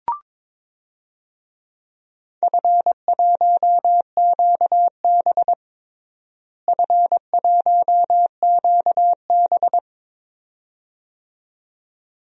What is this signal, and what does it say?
Morse code, heard as a keyed tone, about 22 wpm, 700 Hz. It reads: F1QB F1QB
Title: DXCC-100 Call Signs - 4 Characters - Repeated 2x - 22wpm